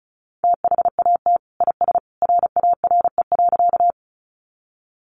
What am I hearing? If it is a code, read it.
Morse code, heard as a keyed tone, about 34 wpm, 700 Hz. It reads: THAT IS RARE.
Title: Sentences from Top 1100 Words 35wpm